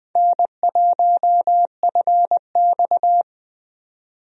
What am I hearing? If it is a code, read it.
Morse code, heard as a keyed tone, about 21 wpm, 700 Hz. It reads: N1FX
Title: USA Call Signs - 4 Characters - 20wpm